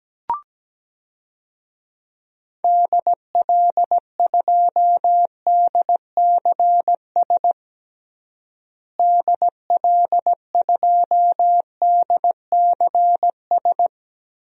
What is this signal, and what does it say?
Morse code, heard as a keyed tone, about 17 wpm, 700 Hz. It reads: DL2DCS DL2DCS
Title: DXCC-100 Call Signs - 6 Characters - Repeated 3x - 17wpm